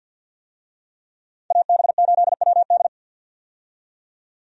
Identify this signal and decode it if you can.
Morse code, heard as a keyed tone, about 52 wpm, 700 Hz. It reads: AB8WD
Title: USA Call Signs - 5 Characters - 50wpm